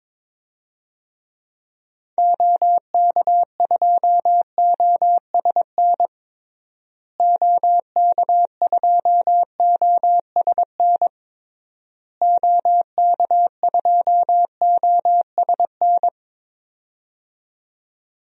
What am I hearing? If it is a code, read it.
Morse code, heard as a keyed tone, about 22 wpm, 700 Hz. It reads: OK2OSN OK2OSN OK2OSN
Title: DXCC-100 Call Signs - 6 Characters - Repeated 3x - 22wpm